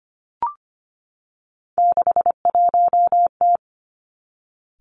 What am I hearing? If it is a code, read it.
Morse code, heard as a keyed tone, about 25 wpm, 700 Hz. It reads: 61T
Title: Three Challenging Letter-Number Combinations 25wpm